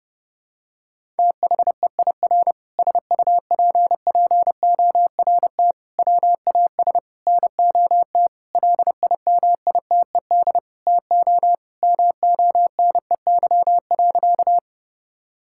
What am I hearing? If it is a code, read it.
Morse code, heard as a keyed tone, about 30 wpm, 700 Hz. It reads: THEIR SUPPORT WAS NOT LIMITED TO MONEY.
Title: Sentences from Top 1000 Words - Review Part 6 - 30wpm